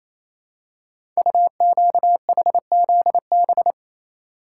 Morse code, heard as a keyed tone, about 27 wpm, 700 Hz. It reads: UQHZB
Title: Code Groups - 5 Letters - 28wpm